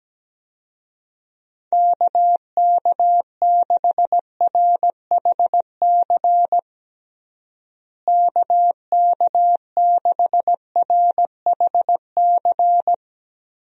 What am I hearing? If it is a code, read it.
Morse code, heard as a keyed tone, about 17 wpm, 700 Hz. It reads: KK6RHC KK6RHC
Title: USA Call Signs - 6 Characters - Repeated 3x - 17wpm